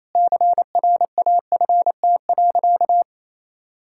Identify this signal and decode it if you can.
Morse code, heard as a keyed tone, about 28 wpm, 700 Hz. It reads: CRAFT.
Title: Sentences from Top 1000 Words - Review Part 6 - 28wpm